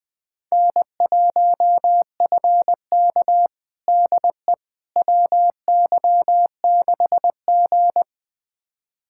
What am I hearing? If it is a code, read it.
Morse code, heard as a keyed tone, about 20 wpm, 700 Hz. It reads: N1FK DE WY6G
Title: USA Callsign DE Callsign - 4 Characters - 20wpm